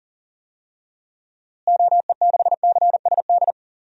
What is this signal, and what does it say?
Morse code, heard as a keyed tone, about 40 wpm, 700 Hz. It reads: OE6CSD
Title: DXCC-100 Call Signs - 6 Characters - Repeated 2x - 40wpm